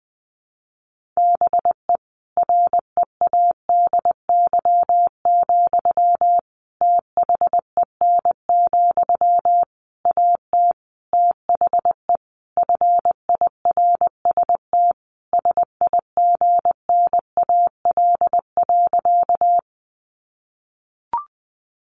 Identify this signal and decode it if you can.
Morse code, heard as a keyed tone, about 19 wpm, 700 Hz. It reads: BE READY, THEN, AT THE FIRST SIGNAL.
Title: Sentences from Top 1400 Words 20wpm